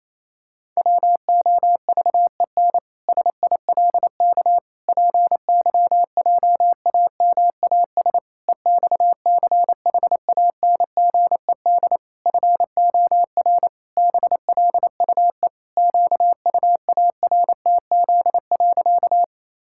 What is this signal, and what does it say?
Morse code, heard as a keyed tone, about 28 wpm, 700 Hz. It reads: WOVEN SILK PYJAMAS EXCHANGED FOR BLUE QUARTZ.
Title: Pangram Challenge v2 - 28wpm